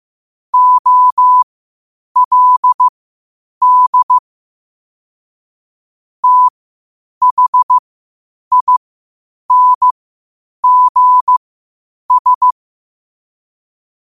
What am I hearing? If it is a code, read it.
Morse code, heard as a keyed tone, about 15 wpm, spaced as for 8 wpm, 1000 Hz. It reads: OLD THINGS
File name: qotdfiles / 8wpm